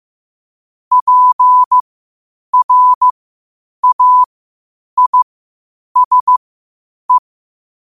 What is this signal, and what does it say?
Morse code, heard as a keyed tone, about 15 wpm, spaced as for 8 wpm, 1000 Hz. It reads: PRAISE
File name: qotdfiles / 8wpm